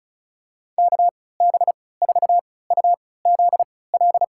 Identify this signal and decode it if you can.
Morse code, heard as a keyed tone, about 36 wpm, spaced as for 20 wpm, 700 Hz. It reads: KB4UZL
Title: USA Call Signs - 3x Character Spacing - 35wpm